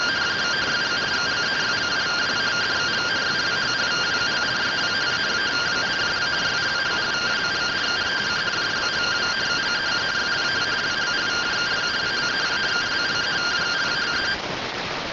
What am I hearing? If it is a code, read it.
Two-tone teleprinter signal, about 170 Hz shift.
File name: findings / MR AXIAL fr1 120.gen-42 rtty rx